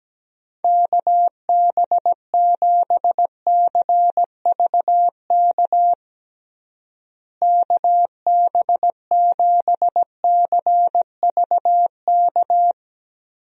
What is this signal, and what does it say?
Morse code, heard as a keyed tone, about 17 wpm, 700 Hz. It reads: KB7CVK KB7CVK
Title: USA Call Signs - 6 Characters - Repeated 2x - 17wpm